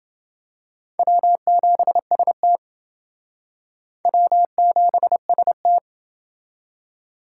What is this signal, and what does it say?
Morse code, heard as a keyed tone, about 28 wpm, 700 Hz. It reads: W7ST W7ST